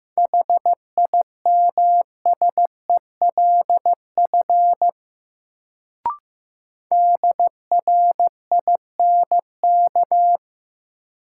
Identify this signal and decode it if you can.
Morse code, heard as a keyed tone, about 15 wpm, 700 Hz. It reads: HIMSELF DRINK
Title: Sentences from Top 1500 Words - New Word Drill Alt - 15wpm